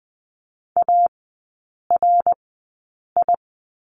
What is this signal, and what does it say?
Morse code, heard as a keyed tone, about 20 wpm, spaced as for 8 wpm, 700 Hz. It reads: ARI
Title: 060 Lesson 11e - Letter h - Three Characters 20wpm